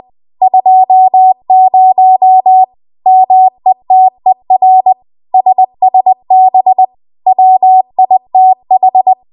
Morse code, heard as a keyed tone, about 20 wpm, 750 Hz. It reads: 20 METER SSB WITH
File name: Archive / 20 WPM / 20WPM